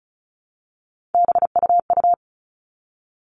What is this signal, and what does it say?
Morse code, heard as a keyed tone, about 34 wpm, 700 Hz. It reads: BUU
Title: All Cut Numbers - 3-Digit 35wpm